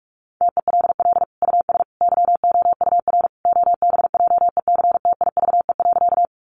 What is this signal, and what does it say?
Morse code, heard as a keyed tone, about 44 wpm, 700 Hz. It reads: TELL US YOUR OBJECTIVE.
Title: Sentences from Top 1200 Words 45wpm